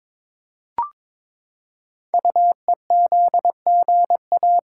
Morse code, heard as a keyed tone, about 22 wpm, 700 Hz. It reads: UEZGA
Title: Code Groups - 5 Letter:Numbers - 22wpm